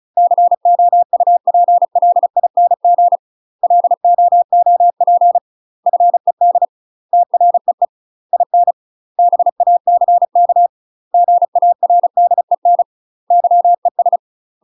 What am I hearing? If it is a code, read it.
Morse code, heard as a keyed tone, about 34 wpm, 700 Hz. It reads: COUPLING LOOP FED TREE IN BACK GARDEN YES
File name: UK MASTER QSOs TRAINING / 35WPM / UK MASTER QSOs 0000